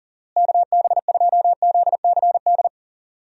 Morse code, heard as a keyed tone, about 40 wpm, 700 Hz. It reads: KB2ZCD
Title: USA Call Signs - 6 Characters - Repeated 2x - 40wpm